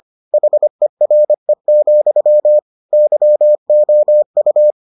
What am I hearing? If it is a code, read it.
Morse code, heard as a keyed tone, about 25 wpm, 600 Hz. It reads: HERE, YOU